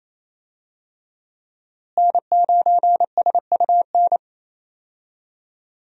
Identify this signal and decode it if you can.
Morse code, heard as a keyed tone, about 28 wpm, 700 Hz. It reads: N9SUN
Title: USA Call Signs - 5 Characters - Repeated 3x - 28wpm